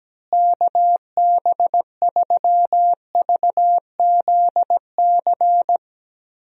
Morse code, heard as a keyed tone, about 17 wpm, 700 Hz. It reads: KB3VZC
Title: USA Call Signs - 6 Characters - Repeated 3x - 17wpm